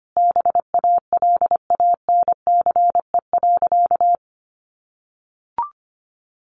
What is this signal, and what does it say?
Morse code, heard as a keyed tone, about 25 wpm, 700 Hz. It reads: BALANCE.
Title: Sentences from Top 900 Words 25wpm